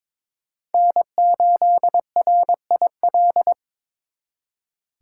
Morse code, heard as a keyed tone, about 26 wpm, 700 Hz. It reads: N8RIL
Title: Speed Racing - USA 1x3 Call Signs 22wpm